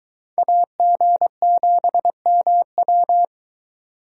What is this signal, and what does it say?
Morse code, heard as a keyed tone, about 23 wpm, 700 Hz. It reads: AG7MW